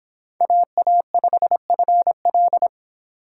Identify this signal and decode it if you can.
Morse code, heard as a keyed tone, about 25 wpm, 700 Hz. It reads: AA5FL